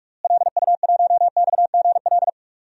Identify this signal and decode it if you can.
Morse code, heard as a keyed tone, about 46 wpm, 700 Hz. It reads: LU1XGL
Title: DXCC-100 Call Signs - 6 Characters - Repeated 3x - 45wpm